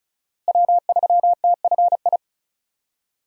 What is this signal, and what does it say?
Morse code, heard as a keyed tone, about 36 wpm, 700 Hz. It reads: W3TFI
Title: USA Call Signs - 5 Characters - Repeated 3x - 35wpm